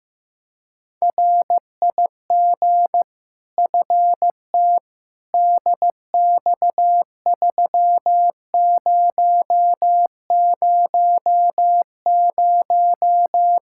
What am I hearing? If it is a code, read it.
Morse code, heard as a keyed tone, about 15 wpm, 700 Hz. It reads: RIG FT DX3000
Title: Practice QSO - Exchange 2 - Info - 15wpm